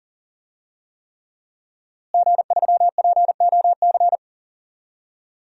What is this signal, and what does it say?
Morse code, heard as a keyed tone, about 40 wpm, 700 Hz. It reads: G3POC
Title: DXCC-100 Call Signs - 5 Characters - Repeated 3x - 40wpm